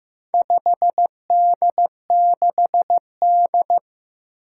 Morse code, heard as a keyed tone, about 15 wpm, 700 Hz. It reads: 5D6D
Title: Four Challenging Letter-Number Combinations 15wpm